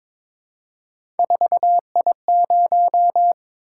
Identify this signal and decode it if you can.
Morse code, heard as a keyed tone, about 22 wpm, 700 Hz. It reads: 4I0
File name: Three Letter-Number Combinations - 22wpm